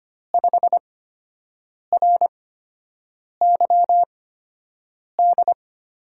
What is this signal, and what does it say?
Morse code, heard as a keyed tone, about 25 wpm, spaced as for 7 wpm, 700 Hz. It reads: 5RYD
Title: USA Call Signs - 6 Characters - 8x Character Spacing - 25wpm